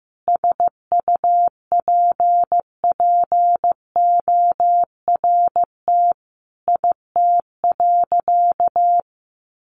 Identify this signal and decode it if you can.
Morse code, heard as a keyed tone, about 15 wpm, 700 Hz. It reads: SUPPORT IT.
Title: Sentences from Top 1300 Words 15wpm